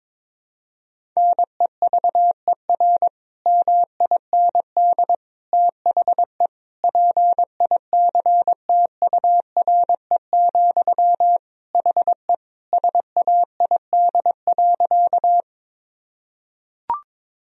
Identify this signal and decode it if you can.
Morse code, heard as a keyed tone, about 22 wpm, 700 Hz. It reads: NEVER MIND THE PICTURE, HE SAID.
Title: Sentences from Top 1000 Words - Review Part 1 - 22wpm